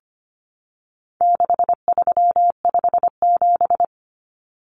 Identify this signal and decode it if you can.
Morse code, heard as a keyed tone, about 25 wpm, 700 Hz. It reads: 6357
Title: Common Cut Numbers - 4-Digit 25wpm